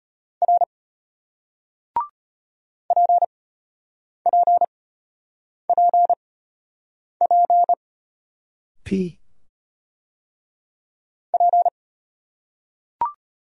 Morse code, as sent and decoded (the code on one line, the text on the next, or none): .-. / .--. .--. .--. .--. / .--.
R PPPP P